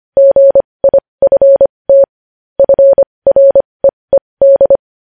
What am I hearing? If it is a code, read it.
Morse code, heard as a keyed tone, about 25 wpm, spaced as for 20 wpm, 550 Hz. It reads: GIFT FREED
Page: news headlines in morse code at 20 wpm